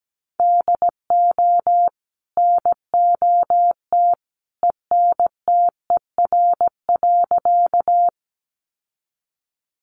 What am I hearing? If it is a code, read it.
Morse code, heard as a keyed tone, about 17 wpm, 700 Hz. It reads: DO NOT ENTER.
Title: Sentences from Top 1300 Words 17wpm